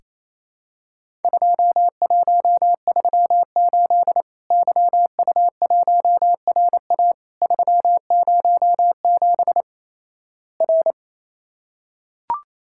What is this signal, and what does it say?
Morse code, heard as a keyed tone, about 28 wpm, 700 Hz. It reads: 2138 YU1RA 307 R
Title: Worked All Europe DX Contest - QTC Exchange - 28wpm